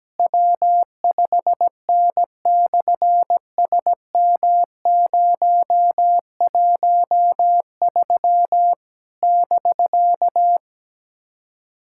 Morse code, heard as a keyed tone, about 17 wpm, 700 Hz. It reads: W5N/SM013 <BK>
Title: SOTA QSO Practice 17wpm